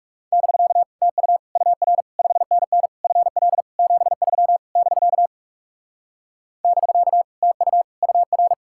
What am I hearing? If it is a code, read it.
Morse code, heard as a keyed tone, about 39 wpm, 700 Hz. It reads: <BK> TU UR 5NN FL 73 <BK> <BK> TU UR